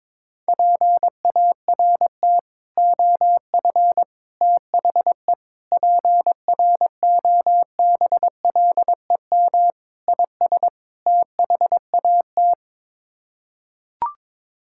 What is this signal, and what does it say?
Morse code, heard as a keyed tone, about 22 wpm, 700 Hz. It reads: PART OF THE PROBLEM IS THAT
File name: practice / 22wpm / Sets of 6 Words - 22wpm